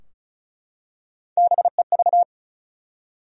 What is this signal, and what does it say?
Morse code, heard as a keyed tone, about 36 wpm, 700 Hz. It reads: BEV
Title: USA Common Names - 35wpm